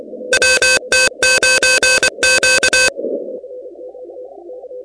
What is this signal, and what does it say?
Morse code, heard as a keyed tone, about 24 wpm, 1550 Hz, with noise. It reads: WT9Q
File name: contests / 2019-cw-wpx-kt5j-k5pi- -2330 0-16